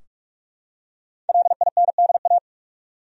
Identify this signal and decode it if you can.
Morse code, heard as a keyed tone, about 47 wpm, 700 Hz. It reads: LINDA
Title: USA Common Names - Max 5 Letters - 45wpm